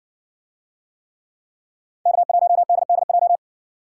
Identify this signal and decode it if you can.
Morse code, heard as a keyed tone, about 60 wpm, 700 Hz. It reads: K1DDJ